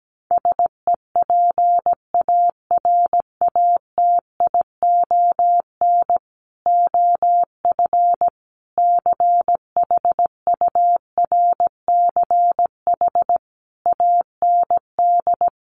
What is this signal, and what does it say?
Morse code, heard as a keyed tone, about 17 wpm, 700 Hz. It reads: SEPARATION OF CHURCH AND
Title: Sets of 5 Words 17wpm